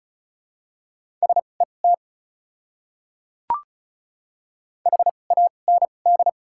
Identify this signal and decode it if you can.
Morse code, heard as a keyed tone, about 36 wpm, spaced as for 25 wpm, 700 Hz. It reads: SET HAND
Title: 2x Character Spacing - Top 200 Words - 35wpm